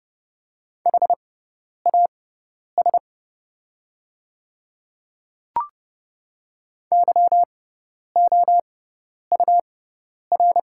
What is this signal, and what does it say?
Morse code, heard as a keyed tone, about 30 wpm, spaced as for 10 wpm, 700 Hz. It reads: HAS YOUR
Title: 6x Character Spacing - Top 100 Words - 30wpm